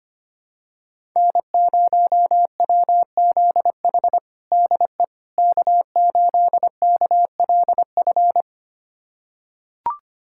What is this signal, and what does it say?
Morse code, heard as a keyed tone, about 25 wpm, 700 Hz. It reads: N0WZH DE K8KLF
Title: USA Callsign DE Callsign - 5 Characters - 25wpm